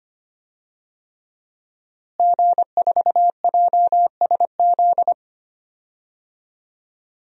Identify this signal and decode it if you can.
Morse code, heard as a keyed tone, about 25 wpm, 700 Hz. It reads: G4JSZ
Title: DXCC-100 Call Signs - 5 Characters - Repeated 3x - 25wpm